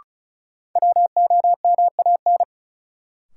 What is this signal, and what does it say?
Morse code, heard as a keyed tone, about 34 wpm, 700 Hz. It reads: WOMAN
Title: Sentences from Top 1100 Words - New Word Drill Alt - 35wpm